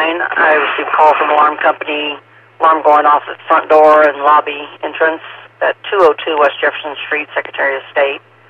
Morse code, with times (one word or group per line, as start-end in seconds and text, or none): none